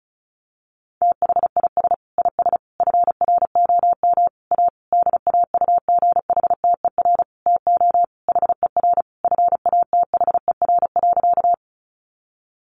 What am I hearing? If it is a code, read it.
Morse code, heard as a keyed tone, about 34 wpm, 700 Hz. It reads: THIS IS FROM A DAUGHTER TO HER FATHER.
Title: Sentences from Top 1500 Words 35wpm